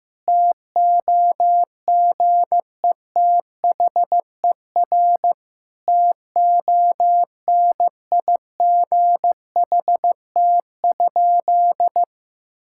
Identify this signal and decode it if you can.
Morse code, heard as a keyed tone, about 15 wpm, 700 Hz. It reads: TOGETHER TONIGHT?
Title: Sentences from Top 1600 Words 15wpm